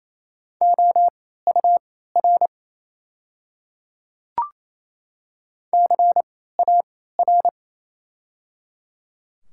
Morse code, heard as a keyed tone, about 28 wpm, spaced as for 16 wpm, 700 Hz. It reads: OUR CAR
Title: 3x Character Spacing - Top 300 Words - 28wpm